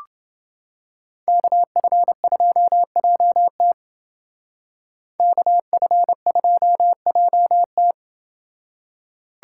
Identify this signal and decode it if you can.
Morse code, heard as a keyed tone, about 30 wpm, 700 Hz. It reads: KF2JT KF2JT